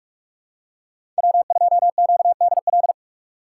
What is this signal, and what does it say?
Morse code, heard as a keyed tone, about 42 wpm, 700 Hz. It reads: W2QDL
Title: USA Call Signs - 5 Characters - 6x Character Spacing - 45wpm